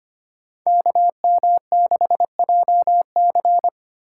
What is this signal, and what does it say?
Morse code, heard as a keyed tone, about 25 wpm, 700 Hz. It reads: KM6JC
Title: Code Groups - 5 Letter:Numbers - 25wpm